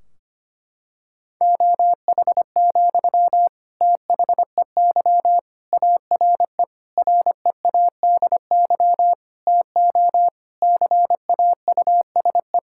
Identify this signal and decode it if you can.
Morse code, heard as a keyed tone, about 25 wpm, 700 Hz. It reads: OH, THEY ARE READY TO CAUSE
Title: Sentences from Top 1600 Words 25wpm